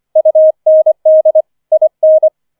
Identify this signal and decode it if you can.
Morse code, heard as a keyed tone, about 24 wpm, 630 Hz. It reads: UND IN
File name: 005 Fünfter Streich 25wpm